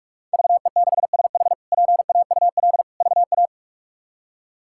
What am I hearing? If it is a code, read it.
Morse code, heard as a keyed tone, about 41 wpm, 700 Hz. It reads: VE6SH PAUL VA
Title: K1USN SST QSO Practice - 45wpm